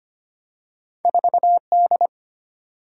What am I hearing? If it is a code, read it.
Morse code, heard as a keyed tone, about 25 wpm, 700 Hz. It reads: 4 D
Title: Single-Letter Single-Number 25wpm